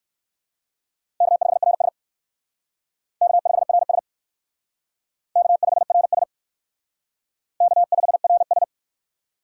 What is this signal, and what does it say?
Morse code, heard as a keyed tone, about 69 wpm, 700 Hz. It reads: K5RS K5RS K5RS K5RS